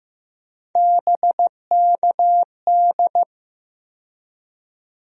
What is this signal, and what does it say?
Morse code, heard as a keyed tone, about 15 wpm, 700 Hz. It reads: BKD
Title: USA Call Sign Postfixes - 3 Letters 15wpm